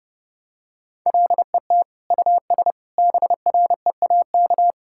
Morse code, heard as a keyed tone, about 30 wpm, 700 Hz. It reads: LET US BREAK
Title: Sentences from Top 1000 Words - Review Part 6 - 30wpm